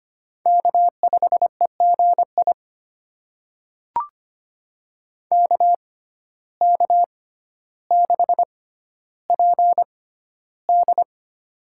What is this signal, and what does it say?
Morse code, heard as a keyed tone, about 25 wpm, spaced as for 13 wpm, 700 Hz. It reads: K5EGI KK6PD
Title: USA Call Signs - 5 Characters - 6x Character Spacing - 25wpm